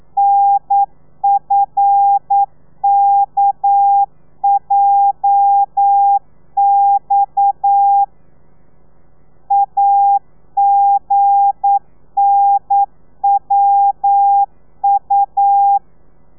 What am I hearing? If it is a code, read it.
Morse code, heard as a keyed tone, about 9 wpm, 800 Hz. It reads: NFKJX AGNWU
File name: telegrafia / i2tfj / Lez8 40